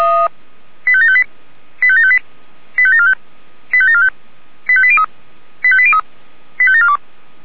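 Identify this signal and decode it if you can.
Five-tone selective call (ZVEI), heard as quick runs of five tones, tones of about 70 ms. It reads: 85758 85758 85745 85745 85803 85803 85742
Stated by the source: ZVEI five-tone selective call received on SDR.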